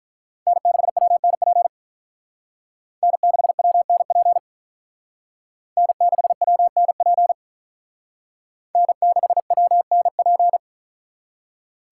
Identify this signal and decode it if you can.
Morse code, heard as a keyed tone, about 42 wpm, 700 Hz. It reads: N6WNP N6WNP N6WNP N6WNP
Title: Speed Racing - USA 1x3 Call Signs 35wpm